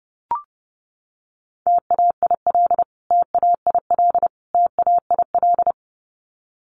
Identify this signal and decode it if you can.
Morse code, heard as a keyed tone, about 29 wpm, 700 Hz. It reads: TAIL TAIL TAIL
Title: Top 500 Words Repeated 30wpm